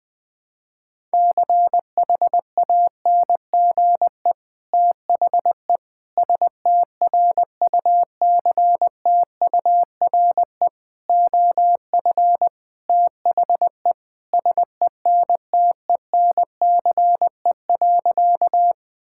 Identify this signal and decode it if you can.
Morse code, heard as a keyed tone, about 20 wpm, 700 Hz. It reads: CHANGE THE STRUCTURE OF THE SENTENCE.
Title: Sentences from Top 1000 Words - Review Part 6 - 20wpm